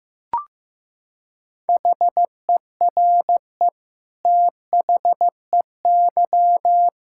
Morse code, heard as a keyed tone, about 15 wpm, 700 Hz. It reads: HERE THEY
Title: Sentences from Top 1600 Words 15wpm